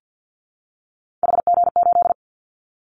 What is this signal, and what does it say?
Morse code, heard as a keyed tone, about 50 wpm, 700 Hz. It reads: HG8
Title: Callsign Prefixes for Top 25 DXCC - 50wpm